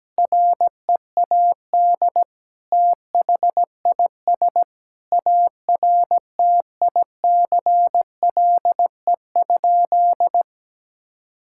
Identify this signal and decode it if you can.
Morse code, heard as a keyed tone, about 17 wpm, 700 Hz. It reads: READ THIS ARTICLE?
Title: Sentences from Top 1000 Words - Review Part 5 - 17wpm